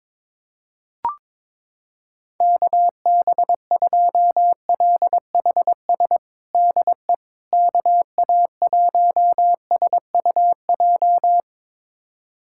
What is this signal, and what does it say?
Morse code, heard as a keyed tone, about 23 wpm, 700 Hz. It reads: KB2LHS DE KA1SUJ
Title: USA Callsign DE Callsign 22wpm